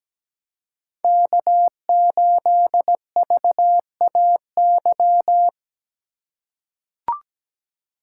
Morse code, heard as a keyed tone, about 17 wpm, 700 Hz. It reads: K8VAY
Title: USA Call Signs - 5 Characters - 2x Character Spacing - 17wpm